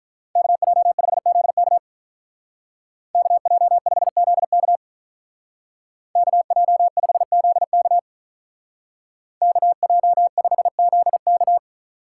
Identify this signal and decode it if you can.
Morse code, heard as a keyed tone, about 47 wpm, 700 Hz. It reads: KJ5ZK KJ5ZK KJ5ZK KJ5ZK